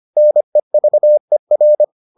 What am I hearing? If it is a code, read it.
Morse code, heard as a keyed tone, about 25 wpm, 600 Hz. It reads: NEVER